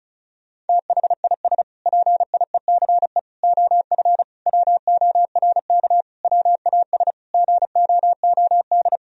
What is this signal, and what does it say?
Morse code, heard as a keyed tone, about 36 wpm, 700 Hz. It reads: THIS PIECE OF WORK WAS GOOD
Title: Sentences from Top 1000 Words - Review Part 2 - 35wpm